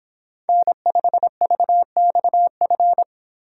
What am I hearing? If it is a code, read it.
Morse code, heard as a keyed tone, about 31 wpm, 700 Hz. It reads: N5VXF